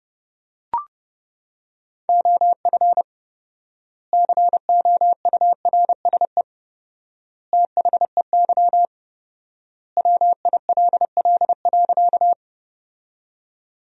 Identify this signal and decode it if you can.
Morse code, heard as a keyed tone, about 32 wpm, 700 Hz. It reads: OF COURSE THEY WILL.
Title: 4x Word-Spacing - Sentences from Top 500 Words - 30wpm